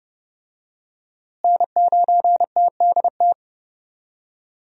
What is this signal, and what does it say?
Morse code, heard as a keyed tone, about 30 wpm, 700 Hz. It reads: N9TDT
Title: USA Call Signs - 5 Characters - Repeated 3x - 30wpm